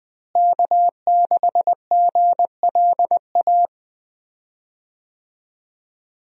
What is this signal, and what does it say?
Morse code, heard as a keyed tone, about 21 wpm, 700 Hz. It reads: K6GLA
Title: USA Call Signs - 5 Characters - Repeated 3x - 20wpm